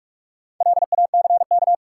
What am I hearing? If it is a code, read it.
Morse code, heard as a keyed tone, about 45 wpm, 700 Hz. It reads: LACK
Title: Sentences from Top 1100 Words - New Word Drill Alt - 45wpm